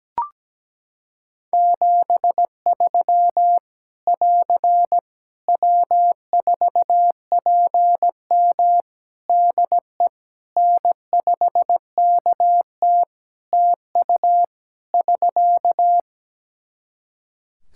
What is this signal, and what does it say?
Morse code, heard as a keyed tone, about 17 wpm, 700 Hz. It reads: 73 <AR> W4PM DE N5KT TU <SK>
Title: K1USN SST QSO Practice 17wpm